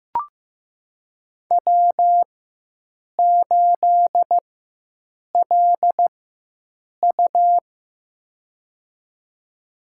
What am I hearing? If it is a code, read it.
Morse code, heard as a keyed tone, about 15 wpm, spaced as for 7 wpm, 700 Hz. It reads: W8LU
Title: USA Call Signs - 4 Characters - 4x Character Spacing - 15wpm